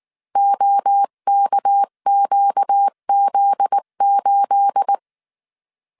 Morse code, heard as a keyed tone, about 19 wpm, 800 Hz. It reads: OKQZ8
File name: CW / DET / 1 9 1500C S C 18wpm 2539